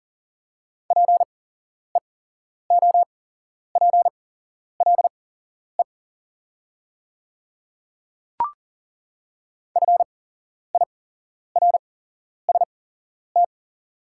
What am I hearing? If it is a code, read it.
Morse code, heard as a keyed tone, about 40 wpm, spaced as for 11 wpm, 690 Hz. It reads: PEOPLE FIRST
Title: 8x Character Spacing - Top 100 Words - 40wpm